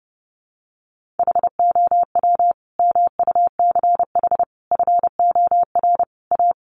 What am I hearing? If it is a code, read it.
Morse code, heard as a keyed tone, about 30 wpm, 700 Hz. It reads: HOW MUCH FOR A
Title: Sentences from Top 1400 Words 30wpm